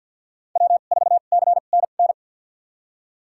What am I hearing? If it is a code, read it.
Morse code, heard as a keyed tone, about 52 wpm, spaced as for 37 wpm, 700 Hz. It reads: W4CNN
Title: USA Call Signs - 5 Characters - 2x Character Spacing - 50wpm